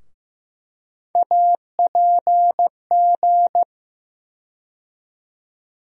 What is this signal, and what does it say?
Morse code, heard as a keyed tone, about 15 wpm, 700 Hz. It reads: APG
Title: USA Call Sign Postfixes - 3 Letters 15wpm